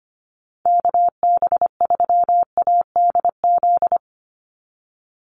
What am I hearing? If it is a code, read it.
Morse code, heard as a keyed tone, about 25 wpm, 700 Hz. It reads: KB3ADZ
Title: USA Call Signs 25wpm